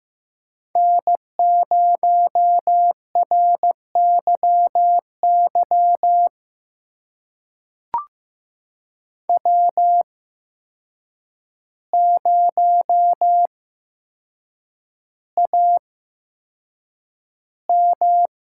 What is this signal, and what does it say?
Morse code, heard as a keyed tone, about 15 wpm, spaced as for 7 wpm, 700 Hz. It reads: N0RYY W0AM
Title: USA Call Signs - 5 Characters - 8x Character Spacing - 15wpm